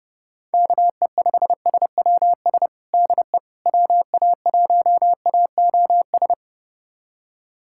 Morse code, heard as a keyed tone, about 30 wpm, 690 Hz. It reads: KE5SWS DE WA1AOS
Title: USA Callsign DE Callsign - 6 Characters - 30wpm